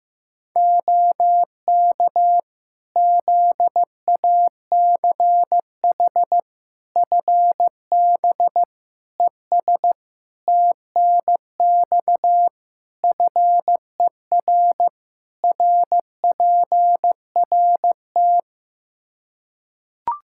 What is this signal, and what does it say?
Morse code, heard as a keyed tone, about 15 wpm, 700 Hz. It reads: OK ZACH FB ES TNX FER RPRT
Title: K1USN SST QSO Practice 15wpm